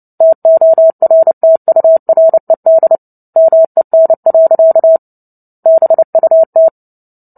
Morse code, heard as a keyed tone, about 30 wpm, 650 Hz. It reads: TORTURED MEN. BUT
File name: Book Two / 03-Chapter 3 - 29 WPM